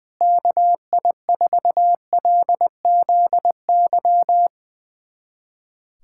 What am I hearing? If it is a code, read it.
Morse code, heard as a keyed tone, about 21 wpm, 700 Hz. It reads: KI4LZY
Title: USA Callsign DE Callsign - 6 Characters - 20wpm